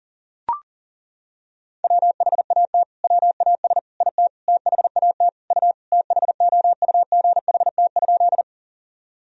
Morse code, heard as a keyed tone, about 40 wpm, 700 Hz. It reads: WHAT WAS IT THAT U THOUGHT?
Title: Sentences from Top 300 Words and Abbreviations 40wpm